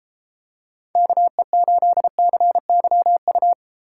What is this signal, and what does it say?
Morse code, heard as a keyed tone, about 32 wpm, 700 Hz. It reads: KE8CYU